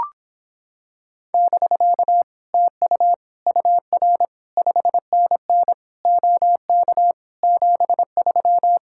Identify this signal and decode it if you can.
Morse code, heard as a keyed tone, about 26 wpm, 700 Hz. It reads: <BK> TU UR 5NN OK 73